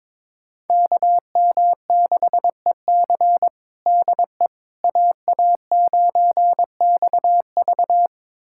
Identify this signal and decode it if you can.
Morse code, heard as a keyed tone, about 22 wpm, 700 Hz. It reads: KM6EC DE AA9XV
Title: USA Callsign DE Callsign - 5 Characters - 22wpm